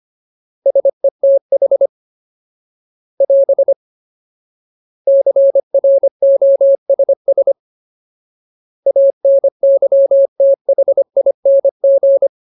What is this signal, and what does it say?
Morse code, heard as a keyed tone, about 25 wpm, 550 Hz. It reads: SETH <AS> CROSS ANYTHING
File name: Ultimate Word List / 25WPM Ultimate